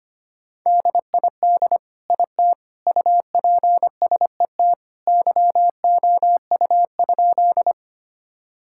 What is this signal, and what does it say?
Morse code, heard as a keyed tone, about 25 wpm, 700 Hz. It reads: DID IT UPSET YOU?
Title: Sentences from Top 1000 Words - Review Part 5 - 25wpm